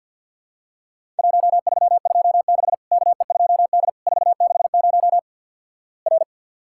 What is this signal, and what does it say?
Morse code, heard as a keyed tone, about 50 wpm, 700 Hz. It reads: 1326 KE2D 460 R
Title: Worked All Europe DX Contest - QTC Exchange with Repeats - 50wpm